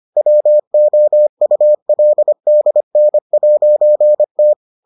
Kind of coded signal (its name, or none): Morse code